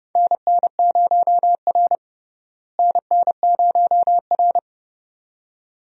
Morse code, heard as a keyed tone, about 30 wpm, 700 Hz. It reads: NN0R NN0R
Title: USA Call Signs - 4 Characters - Repeated 2x - 30wpm